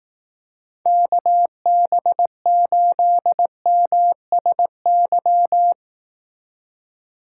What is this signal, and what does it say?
Morse code, heard as a keyed tone, about 18 wpm, 700 Hz. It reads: KB8MSY
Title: Speed Racing - USA Call Signs 15wpm